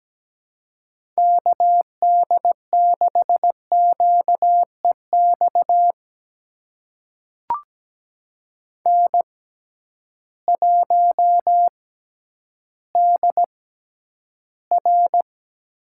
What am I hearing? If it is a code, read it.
Morse code, heard as a keyed tone, about 17 wpm, spaced as for 10 wpm, 700 Hz. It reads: KD6QEX N1DR
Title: USA Call Signs - 6x Character Spacing - 17wpm